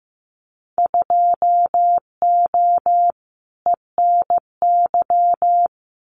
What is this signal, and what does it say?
Morse code, heard as a keyed tone, about 15 wpm, 700 Hz. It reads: 2O ENY
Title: Winter Field Day Exchanges Encore 15wpm